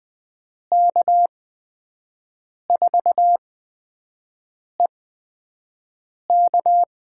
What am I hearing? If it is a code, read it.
Morse code, heard as a keyed tone, about 20 wpm, spaced as for 5 wpm, 700 Hz. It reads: K4EK
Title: USA Call Signs - 5 Characters - 8x Character Spacing - 20wpm